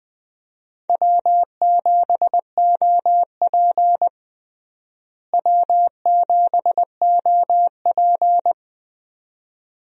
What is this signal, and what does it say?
Morse code, heard as a keyed tone, about 20 wpm, 700 Hz. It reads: W7OP W7OP
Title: USA Call Signs - 4 Characters - Repeated 2x - 20wpm